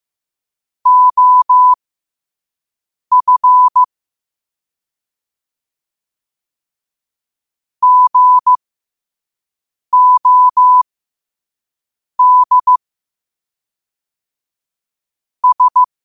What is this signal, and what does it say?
Morse code, heard as a keyed tone, about 15 wpm, spaced as for 5 wpm, 1000 Hz. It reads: OF GOD S
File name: qotdfiles / 5wpm